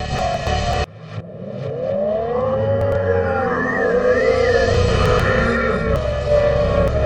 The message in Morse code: none